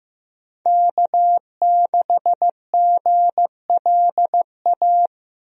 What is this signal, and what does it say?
Morse code, heard as a keyed tone, about 16 wpm, 700 Hz. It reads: K6GLA
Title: USA Call Signs - 5 Characters - Repeated 3x - 15wpm